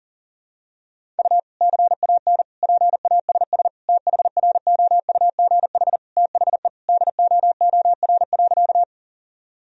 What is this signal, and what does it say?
Morse code, heard as a keyed tone, about 40 wpm, 700 Hz. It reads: U CAN PASS THROUGH THE DOOR.
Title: Sentences from Top 400 Words and Abbreviations 40wpm